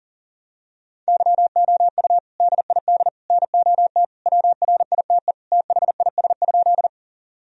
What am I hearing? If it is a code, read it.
Morse code, heard as a keyed tone, about 40 wpm, 700 Hz. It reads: YOU DID NOT WRITE THIS?
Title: Sentences from Top 1000 Words - Review Part 1 - 40wpm